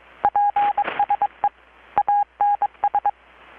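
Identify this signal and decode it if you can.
Morse code, heard as a keyed tone, about 23 wpm, 820 Hz, with radio noise. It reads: PSE ANS
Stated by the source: Morse from KPH's Night of Nights XX maritime radiotelegraphy event, received in Chiapas, Mexico about 3,800 km away on a Kenwood R-600 with a grounded 10 m long-wire antenna.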